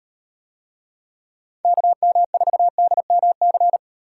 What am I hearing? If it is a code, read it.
Morse code, heard as a keyed tone, about 35 wpm, 700 Hz. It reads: KM4DMC